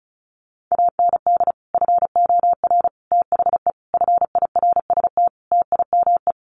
Morse code, heard as a keyed tone, about 34 wpm, 700 Hz. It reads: AND FOR THE FIRST TIME
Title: Sets of 5 Words 35wpm